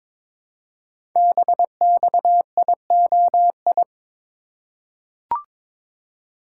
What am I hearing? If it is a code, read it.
Morse code, heard as a keyed tone, about 22 wpm, 700 Hz. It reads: BXIOI
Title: Code Groups - 5 Letters - 22wpm